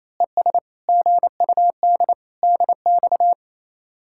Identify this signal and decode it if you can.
Morse code, heard as a keyed tone, about 28 wpm, 700 Hz. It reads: ES GUD DX
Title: Speed-Racing Common QSO Phrases - 28wpm